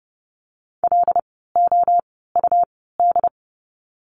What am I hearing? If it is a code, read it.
Morse code, heard as a keyed tone, about 30 wpm, spaced as for 17 wpm, 700 Hz. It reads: LOUD
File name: Lesson 17f - Letter u - Words 30wpm